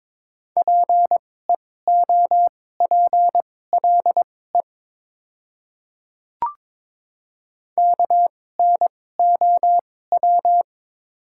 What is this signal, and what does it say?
Morse code, heard as a keyed tone, about 22 wpm, spaced as for 16 wpm, 700 Hz. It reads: PEOPLE KNOW
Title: 2x Character Spacing - Top 100 Words - 22wpm